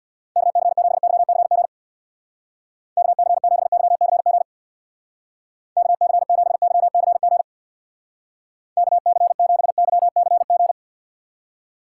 Morse code, heard as a keyed tone, about 65 wpm, 700 Hz. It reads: KC7YCG KC7YCG KC7YCG KC7YCG